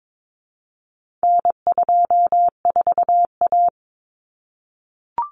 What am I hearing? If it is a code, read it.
Morse code, heard as a keyed tone, about 22 wpm, 700 Hz. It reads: N24A
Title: Common Cut Numbers - 4-Digit 22wpm